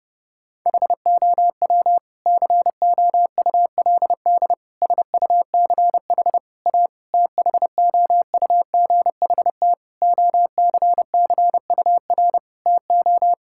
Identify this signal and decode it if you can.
Morse code, heard as a keyed tone, about 32 wpm, 700 Hz. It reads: HOW COULD SUCH A THOUGHT OCCUR TO
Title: Sentences from Top 1000 Words - Review Part 6 - 30wpm